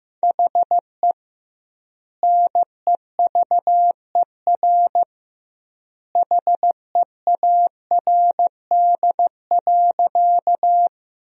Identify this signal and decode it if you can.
Morse code, heard as a keyed tone, about 15 wpm, 700 Hz. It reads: HE NEVER HEARD.
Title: 2x Word-Spacing - Sentences from Top 400 Words - 15wpm